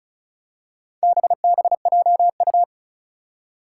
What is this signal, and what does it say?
Morse code, heard as a keyed tone, about 36 wpm, 700 Hz. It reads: BBJU
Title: Four Challenging Letter-Number Combinations 35wpm